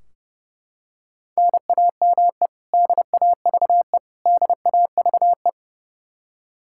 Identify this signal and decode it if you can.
Morse code, heard as a keyed tone, about 30 wpm, 700 Hz. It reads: NAME DAVE DAVE
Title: Practice QSO - Exchange 1 - Beginning - 30wpm